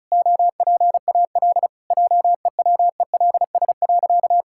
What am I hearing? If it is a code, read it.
Morse code, heard as a keyed tone, about 36 wpm, 700 Hz. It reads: OPAL JEWELS.
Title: Pangram Challenge v2 - 35wpm